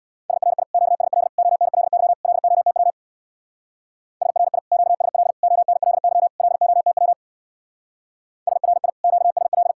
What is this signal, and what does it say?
Morse code, heard as a keyed tone, about 60 wpm, 700 Hz. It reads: SRI QSL ONLY LOTW SRI QSL ONLY LOTW SRI QSL
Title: Speed-Racing Common QSO Phrases - 50wpm